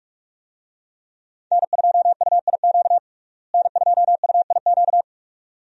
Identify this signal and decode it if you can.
Morse code, heard as a keyed tone, about 45 wpm, 700 Hz. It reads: N2UIQ N2UIQ
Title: USA Call Signs - 5 Characters - Repeated 2x - 45wpm